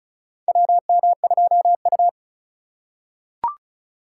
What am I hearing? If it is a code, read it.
Morse code, heard as a keyed tone, about 36 wpm, 700 Hz. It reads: WM2U
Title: USA Call Signs - 4 Characters - 4x Character Spacing - 35wpm